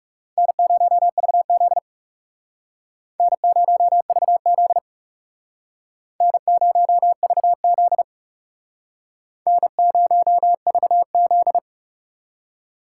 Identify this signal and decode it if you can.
Morse code, heard as a keyed tone, about 36 wpm, 700 Hz. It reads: N0VZ N0VZ N0VZ N0VZ
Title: Speed Racing - USA Call Signs 30wpm